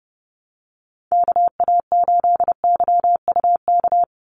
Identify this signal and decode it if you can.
Morse code, heard as a keyed tone, about 30 wpm, 700 Hz. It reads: KA8YUK
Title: USA Call Signs 30wpm